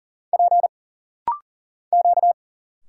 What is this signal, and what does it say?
Morse code, heard as a keyed tone, about 40 wpm, 700 Hz. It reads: P Q
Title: Single Letters - Rapid-Fire 40wpm